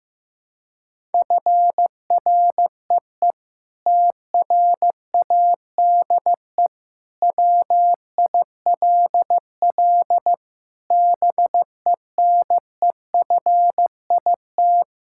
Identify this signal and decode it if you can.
Morse code, heard as a keyed tone, about 16 wpm, 700 Hz. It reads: FREE TRADE WILL BENEFIT
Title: Sentences from Top 1000 Words - Review Part 6 - 15wpm